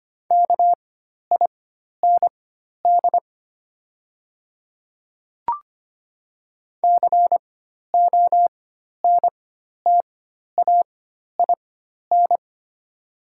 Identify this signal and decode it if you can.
Morse code, heard as a keyed tone, about 25 wpm, spaced as for 12 wpm, 690 Hz. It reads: KIND CONTAIN
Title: 4x Character Spacing - Top 500 Words - 25wpm